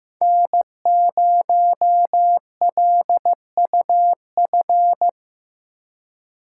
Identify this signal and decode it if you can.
Morse code, heard as a keyed tone, about 18 wpm, 700 Hz. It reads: N0LUF
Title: Speed Racing - USA 1x3 Call Signs 15wpm